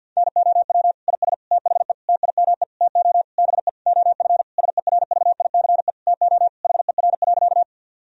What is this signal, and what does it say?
Morse code, heard as a keyed tone, about 50 wpm, 700 Hz. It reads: NOW IS THE TIME TO BE OF SERVICE TO HER.
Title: Sentences from Top 1000 Words - Review Part 4 - 50wpm